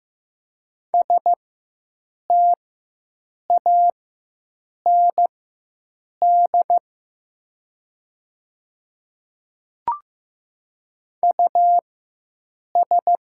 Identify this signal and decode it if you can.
Morse code, heard as a keyed tone, about 15 wpm, spaced as for 7 wpm, 700 Hz. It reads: STAND US
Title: 4x Character Spacing - Top 500 Words - 15wpm